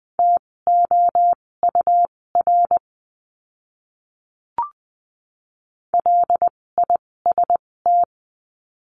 Morse code, heard as a keyed tone, about 20 wpm, spaced as for 16 wpm, 700 Hz. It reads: TOUR LIST
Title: 256 Lesson 42a - Words 12-20wpm